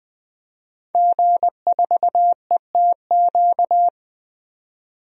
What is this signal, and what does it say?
Morse code, heard as a keyed tone, about 21 wpm, 700 Hz. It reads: G4ETQ
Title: DXCC-100 Call Signs - 5 Characters - 20wpm